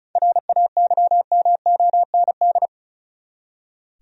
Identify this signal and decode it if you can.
Morse code, heard as a keyed tone, about 34 wpm, 700 Hz. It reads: RAYMOND
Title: USA Common Names - 35wpm